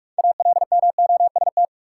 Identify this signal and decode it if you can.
Morse code, heard as a keyed tone, about 45 wpm, 700 Hz. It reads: ALMOST
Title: Sentences from Top 700 Words - New Word Drill - 45wpm